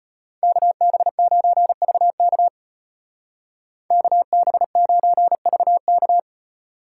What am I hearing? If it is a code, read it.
Morse code, heard as a keyed tone, about 33 wpm, 700 Hz. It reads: KB9VK KB9VK